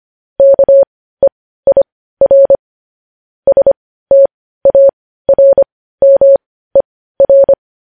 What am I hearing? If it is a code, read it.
Morse code, heard as a keyed tone, about 25 wpm, spaced as for 15 wpm, 550 Hz. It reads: KEIR STARMER
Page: News headlines in morse code at 15 wpm